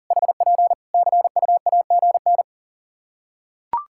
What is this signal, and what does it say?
Morse code, heard as a keyed tone, about 40 wpm, 700 Hz. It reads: HP CUAGN
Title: Practice QSO - Exchange 4 - Ending - 40wpm